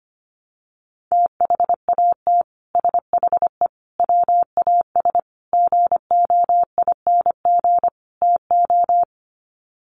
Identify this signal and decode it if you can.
Morse code, heard as a keyed tone, about 25 wpm, 700 Hz. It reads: THAT SHE WAS GOING TO
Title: Sets of 5 Words Encore2 25wpm